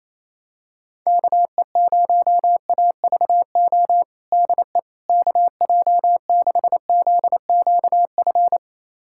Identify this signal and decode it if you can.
Morse code, heard as a keyed tone, about 27 wpm, 700 Hz. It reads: KE0AVO DE KJ6ZQF
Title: USA Callsign DE Callsign 28wpm